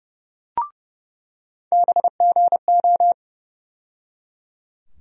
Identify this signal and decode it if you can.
Morse code, heard as a keyed tone, about 31 wpm, 700 Hz. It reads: BGO
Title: USA Call Sign Postfixes - 3 Letters 30wpm